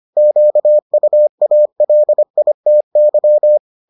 Morse code, heard as a keyed tone, about 25 wpm, 600 Hz. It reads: QUALITY